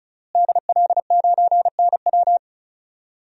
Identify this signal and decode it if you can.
Morse code, heard as a keyed tone, about 36 wpm, 700 Hz. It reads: DL9NW
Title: DXCC-100 Call Signs - 5 Characters - Repeated 2x - 35wpm